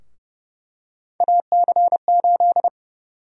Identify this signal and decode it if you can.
Morse code, heard as a keyed tone, about 30 wpm, 700 Hz. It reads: AC8
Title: USA Call Sign Prefixes - Repeated 2x - 30wpm